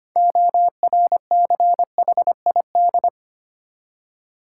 Morse code, heard as a keyed tone, about 25 wpm, 690 Hz. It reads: ORCHID
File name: practice / 25wpm / Easy Words with Hints - 25wpm